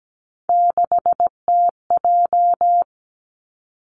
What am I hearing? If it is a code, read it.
Morse code, heard as a keyed tone, about 17 wpm, 700 Hz. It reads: 6TJ
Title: Three Challenging Letter-Number Combinations 17wpm